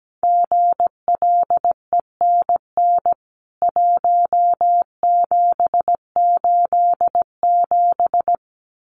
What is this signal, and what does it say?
Morse code, heard as a keyed tone, about 17 wpm, 700 Hz. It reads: GLENN 1787
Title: CWT Exchange 17wpm